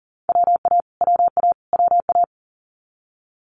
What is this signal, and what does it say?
Morse code, heard as a keyed tone, about 42 wpm, 700 Hz. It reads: WA WA WA
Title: Multi-letter Phonemes Repeated 3x 40wpm